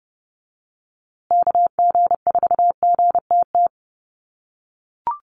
Morse code, heard as a keyed tone, about 30 wpm, 700 Hz. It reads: KG4GTT
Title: USA Call Signs Repeated 2x 30wpm